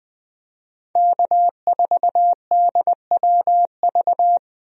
Morse code, heard as a keyed tone, about 20 wpm, 700 Hz. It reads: K4DWV
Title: USA Call Signs - 5 Characters - Repeated 2x - 20wpm